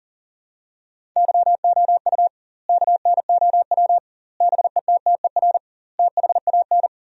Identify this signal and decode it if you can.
Morse code, heard as a keyed tone, about 40 wpm, 700 Hz. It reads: YOU KNOW BETTER THAN
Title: 2x Word-Spacing - Sentences from Top 400 Words - 40wpm